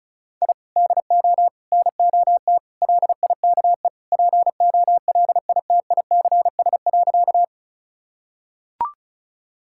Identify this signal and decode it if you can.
Morse code, heard as a keyed tone, about 34 wpm, 700 Hz. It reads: I DO NOT LIKE POLITICS.
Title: Sentences from Top 1000 Words - Review Part 6 - 35wpm